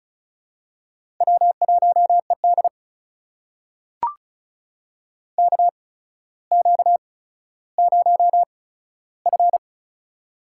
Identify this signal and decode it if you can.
Morse code, heard as a keyed tone, about 35 wpm, spaced as for 15 wpm, 700 Hz. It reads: W1ED KQ0F
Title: USA Call Signs - 4 Characters - 8x Character Spacing - 35wpm